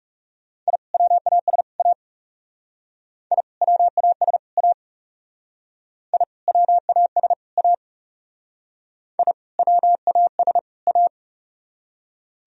Morse code, heard as a keyed tone, about 40 wpm, 700 Hz. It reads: I WAS A I WAS A I WAS A I WAS A